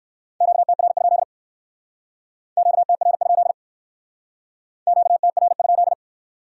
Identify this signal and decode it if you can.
Morse code, heard as a keyed tone, about 67 wpm, 700 Hz. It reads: QTR? QTR? QTR?